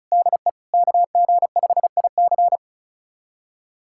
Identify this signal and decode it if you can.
Morse code, heard as a keyed tone, about 36 wpm, 700 Hz. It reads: DE KG5IC
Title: USA Callsign DE Callsign - 5 Characters - 35wpm